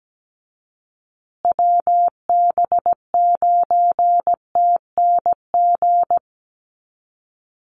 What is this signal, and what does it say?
Morse code, heard as a keyed tone, about 17 wpm, 700 Hz. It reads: WB9TNG
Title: USA Call Signs Repeated 3x 17wpm